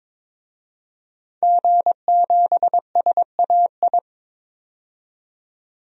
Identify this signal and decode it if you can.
Morse code, heard as a keyed tone, about 22 wpm, 700 Hz. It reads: G7SAI
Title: DXCC-100 Call Signs - 5 Characters - 22wpm